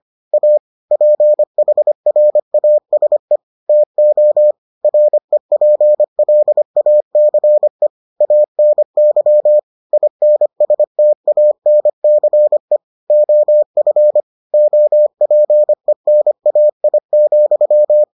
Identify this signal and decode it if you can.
Morse code, heard as a keyed tone, about 25 wpm, 600 Hz. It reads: A PHRASE TO REPLACE ANY INSTANCE OF OPENAI,